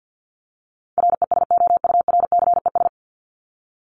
Morse code, heard as a keyed tone, about 50 wpm, 700 Hz. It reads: RESOURCES
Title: ES:S Suffix - Top 100 Words with Base Words 50wpm